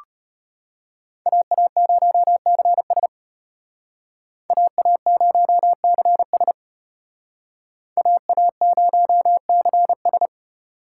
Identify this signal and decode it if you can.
Morse code, heard as a keyed tone, about 32 wpm, 700 Hz. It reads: AA0CS AA0CS AA0CS